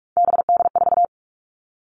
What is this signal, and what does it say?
Morse code, heard as a keyed tone, about 46 wpm, 700 Hz. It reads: BD4
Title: Callsign Prefixes for Top 100 DXCC - 45wpm